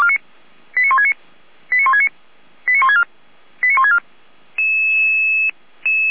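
Five-tone selective call (ZVEI), heard as quick runs of five tones, tones of about 70 ms. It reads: 89169 89169 89164 89164
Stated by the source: ZVEI five-tone selective call received on SDR.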